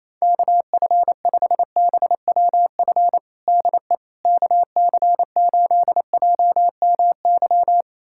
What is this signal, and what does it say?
Morse code, heard as a keyed tone, about 28 wpm, 700 Hz. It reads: KF5BWF DE KC8JMY
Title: USA Callsign DE Callsign - 6 Characters - 28wpm